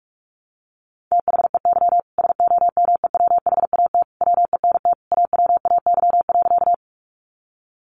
Morse code, heard as a keyed tone, about 44 wpm, 700 Hz. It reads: THEY SOMEWHAT WENT AWAY.
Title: Sentences from Top 1300 Words 45wpm